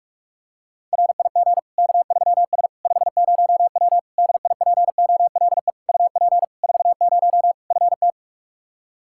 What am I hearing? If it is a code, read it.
Morse code, heard as a keyed tone, about 41 wpm, 700 Hz. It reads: RIG K3S 50W DIPOLE UP 40 FT
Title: K1USN SST QSO Practice 45wpm